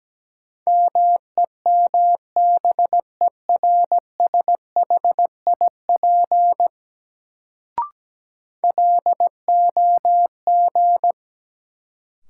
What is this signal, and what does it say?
Morse code, heard as a keyed tone, about 17 wpm, 700 Hz. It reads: MEMBERSHIP LOG
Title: Sentences from Top 1400 Words - New Word Drill - 17wpm